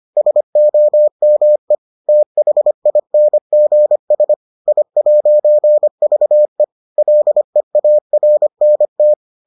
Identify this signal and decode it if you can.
Morse code, heard as a keyed tone, about 25 wpm, 600 Hz. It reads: SOME THINGS I'VE LEARNT